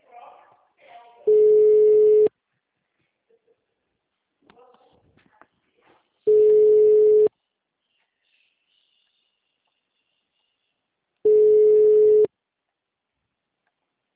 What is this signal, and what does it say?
Ringback tone (Europe, 425 Hz), with speech.